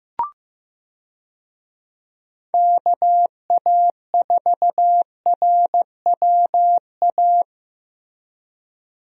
Morse code, heard as a keyed tone, about 16 wpm, 700 Hz. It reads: KA4RWA
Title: USA Call Signs - 6 Characters - Repeated 2x - 15wpm